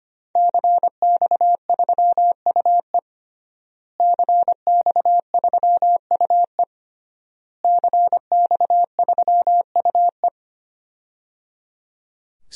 Morse code, heard as a keyed tone, about 25 wpm, 700 Hz. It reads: CX3UE CX3UE CX3UE
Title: DXCC-100 Call Signs - 5 Characters - Repeated 3x - 25wpm